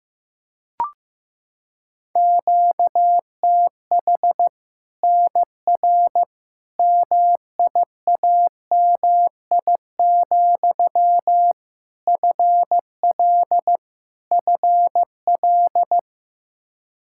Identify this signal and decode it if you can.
Morse code, heard as a keyed tone, about 14 wpm, 700 Hz. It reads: QTH NR MIAMI, FL FL
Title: Practice QSO - Exchange 1 - Beginning - 15wpm